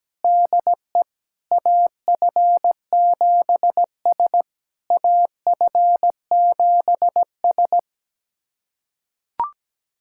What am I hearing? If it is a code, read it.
Morse code, heard as a keyed tone, about 17 wpm, 690 Hz. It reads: DE AF7S AF7S
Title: SOTA QSO Practice 17wpm